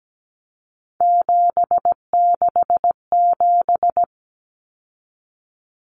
Morse code, heard as a keyed tone, about 17 wpm, 700 Hz. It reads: 767
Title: Numbers - 3-Digit 17wpm